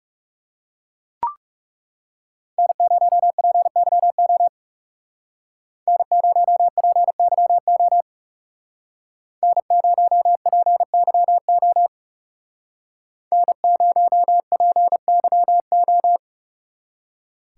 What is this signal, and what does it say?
Morse code, heard as a keyed tone, about 36 wpm, 700 Hz. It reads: N0PYO N0PYO N0PYO N0PYO
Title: Speed Racing - USA 1x3 Call Signs 30wpm